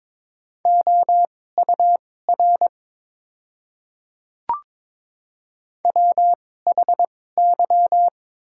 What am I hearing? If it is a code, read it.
Morse code, heard as a keyed tone, about 22 wpm, spaced as for 16 wpm, 700 Hz. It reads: OUR WHY
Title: 2x Character Spacing - Top 200 Words - 22wpm